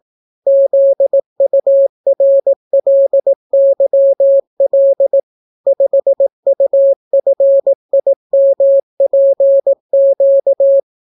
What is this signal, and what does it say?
Morse code, heard as a keyed tone, about 18 wpm, 550 Hz. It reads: ZURLYL 5UFIMPQ